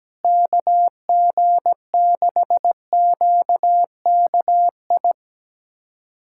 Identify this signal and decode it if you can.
Morse code, heard as a keyed tone, about 17 wpm, 700 Hz. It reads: KG6QKI
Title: USA Callsign DE Callsign 17wpm